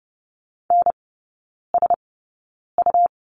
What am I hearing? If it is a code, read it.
Morse code, heard as a keyed tone, about 30 wpm, spaced as for 9 wpm, 700 Hz. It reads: NSU
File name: Lesson 18e - Letter m - Three Characters 30wpm